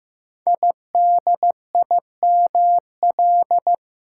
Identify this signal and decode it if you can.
Morse code, heard as a keyed tone, about 15 wpm, 700 Hz. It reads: IDIML
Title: Code Groups - 5 Letter:Numbers - 15wpm